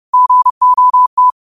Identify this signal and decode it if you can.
Morse code, heard as a keyed tone, about 30 wpm, 1000 Hz. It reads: GOT
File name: qotdfiles / 30wpm